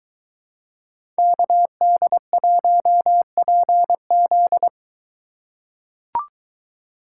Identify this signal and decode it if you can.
Morse code, heard as a keyed tone, about 23 wpm, 700 Hz. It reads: KD1PZ